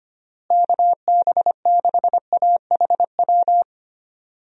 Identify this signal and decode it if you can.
Morse code, heard as a keyed tone, about 25 wpm, 700 Hz. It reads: KB6AHW
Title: USA Call Signs - 6 Characters - Repeated 3x - 25wpm